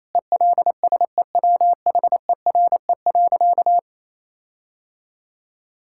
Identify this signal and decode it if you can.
Morse code, heard as a keyed tone, about 28 wpm, 700 Hz. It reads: ELSEWHERE.
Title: Sentences from Top 1600 Words 28wpm